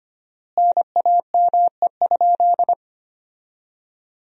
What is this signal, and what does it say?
Morse code, heard as a keyed tone, about 25 wpm, 700 Hz. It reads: NAME?
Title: 4x Word-Spacing - Sentences from Top 300 Words - 25wpm